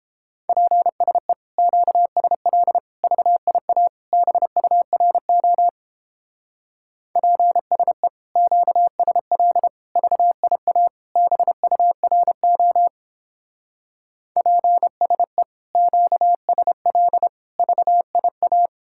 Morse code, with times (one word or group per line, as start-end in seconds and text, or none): 0.5-1.3 PSE
1.6-2.8 QSL
3.0-3.9 VIA
4.1-5.7 BURO
7.2-8.1 PSE
8.4-9.7 QSL
10.0-10.9 VIA
11.2-12.9 BURO
14.4-15.4 PSE
15.8-17.3 QSL
17.6-18.7 VIA